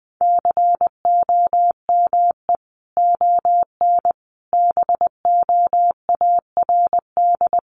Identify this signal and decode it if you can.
Morse code, heard as a keyed tone, about 20 wpm, 700 Hz. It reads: COME ON BOARD
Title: Sentences from Top 1300 Words 20wpm